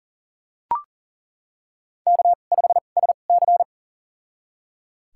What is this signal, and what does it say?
Morse code, heard as a keyed tone, about 38 wpm, spaced as for 28 wpm, 690 Hz. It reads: K5SC
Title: USA Call Signs - 4 Characters - 2x Character Spacing - 40wpm